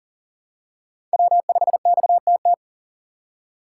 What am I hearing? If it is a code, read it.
Morse code, heard as a keyed tone, about 38 wpm, 700 Hz. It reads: W5XTT
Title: USA Call Signs - 5 Characters - 2x Character Spacing - 40wpm